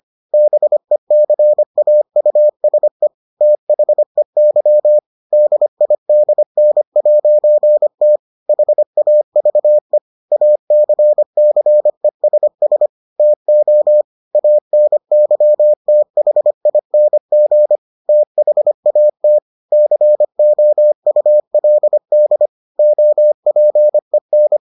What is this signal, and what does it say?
Morse code, heard as a keyed tone, about 25 wpm, 600 Hz. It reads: BECAUSE THEY DIDN'T HAVE ACCESS TO ANYTHING THAT COULD OPEN